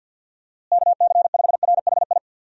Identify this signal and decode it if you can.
Morse code, heard as a keyed tone, about 50 wpm, 700 Hz. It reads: KC5RHI
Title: USA Call Signs - 6 Characters - Repeated 3x - 50wpm